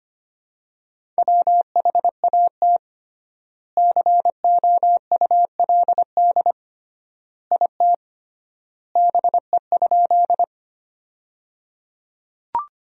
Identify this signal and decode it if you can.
Morse code, heard as a keyed tone, about 25 wpm, 700 Hz. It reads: WHAT COULD IT BE?
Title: 3x Word-Spacing - Sentences from Top 100 Words - 25wpm